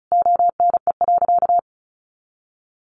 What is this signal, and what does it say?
Morse code, heard as a keyed tone, about 34 wpm, 700 Hz. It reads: ONE.
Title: Sentences from Top 300 Words 35wpm